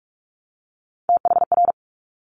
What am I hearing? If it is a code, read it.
Morse code, heard as a keyed tone, about 46 wpm, 690 Hz. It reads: THR
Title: Top Three Letter Combinations in Words 45wpm